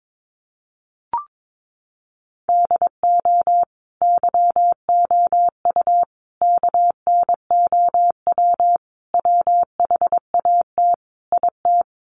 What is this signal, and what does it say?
Morse code, heard as a keyed tone, about 22 wpm, 700 Hz. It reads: DO YOU KNOW WHAT IT
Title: Sentences from Top 100 Words 22wpm